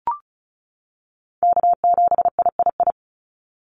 Morse code, heard as a keyed tone, about 36 wpm, 700 Hz. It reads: K7III
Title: NA SOTA Chaser Callsigns 35wpm